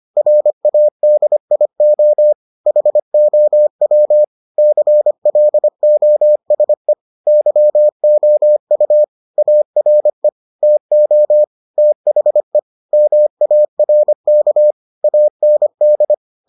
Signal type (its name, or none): Morse code